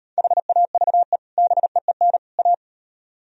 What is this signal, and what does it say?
Morse code, heard as a keyed tone, about 35 wpm, 700 Hz. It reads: HAVE BEEN A